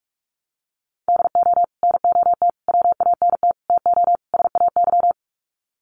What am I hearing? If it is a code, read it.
Morse code, heard as a keyed tone, about 44 wpm, 700 Hz. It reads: DO NOT WANT TO SAY
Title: Sets of 5 Words Encore 45wpm